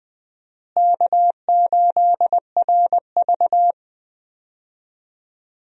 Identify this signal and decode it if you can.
Morse code, heard as a keyed tone, about 20 wpm, 700 Hz. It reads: K8RV
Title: USA Call Signs - 4 Characters - Repeated 2x - 20wpm